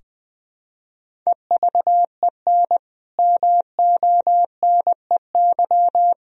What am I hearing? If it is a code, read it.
Morse code, heard as a keyed tone, about 21 wpm, 700 Hz. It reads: EVEN MONEY
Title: Sentences from Top 1000 Words - Review Part 4 - 20wpm